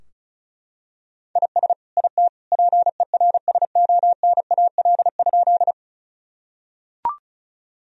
Morse code, heard as a keyed tone, about 36 wpm, 700 Hz. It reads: IS IT PERSONAL?
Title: Sentences from Top 1000 Words - Review Part 4 - 35wpm